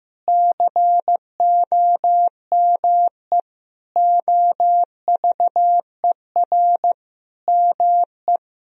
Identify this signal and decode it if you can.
Morse code, heard as a keyed tone, about 15 wpm, 700 Hz. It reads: COME OVER ME
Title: Sentences from Top 1000 Words - Review Part 4 - 15wpm